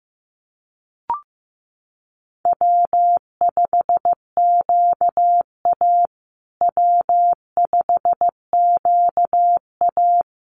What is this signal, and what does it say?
Morse code, heard as a keyed tone, about 15 wpm, 700 Hz. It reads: W5QA W5QA
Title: USA Call Signs Repeated 2x 15wpm